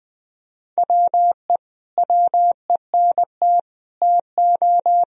Morse code, heard as a keyed tone, about 20 wpm, 700 Hz. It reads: WE WENT TO
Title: Sentences from Top 1000 Words - Review Part 3 - 20wpm